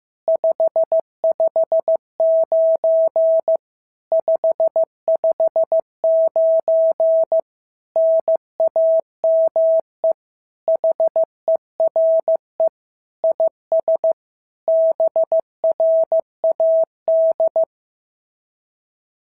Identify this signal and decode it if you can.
Morse code, heard as a keyed tone, about 14 wpm, 650 Hz. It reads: 559 559 NAME HERE IS BRAD
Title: SOTA QSO Practice 15wpm